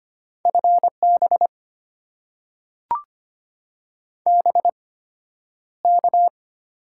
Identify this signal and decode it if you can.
Morse code, heard as a keyed tone, about 25 wpm, spaced as for 11 wpm, 690 Hz. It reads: FB BK
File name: Top 90 QSO Elements - 8x Character Spacing - 25wpm